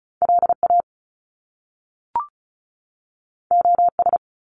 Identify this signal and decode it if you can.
Morse code, heard as a keyed tone, about 36 wpm, 700 Hz. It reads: LA OS
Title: Top Two Letter Combinations in Words 35wpm